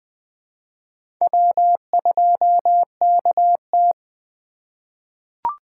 Morse code, heard as a keyed tone, about 20 wpm, 700 Hz. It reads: W2KT
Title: USA Call Signs - 4 Characters - 3x Character Spacing - 20wpm